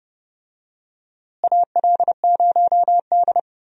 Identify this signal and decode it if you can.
Morse code, heard as a keyed tone, about 30 wpm, 700 Hz. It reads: AL0D
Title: USA Call Signs - 4 Characters - Repeated 2x - 30wpm